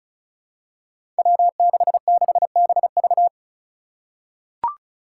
Morse code, heard as a keyed tone, about 36 wpm, 700 Hz. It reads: W66BV
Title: Five Challenging Letter-Number Combinations 35wpm